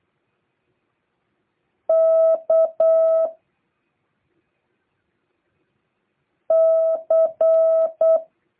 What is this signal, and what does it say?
Morse code, heard as a keyed tone, about 8 wpm, 650 Hz. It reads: K C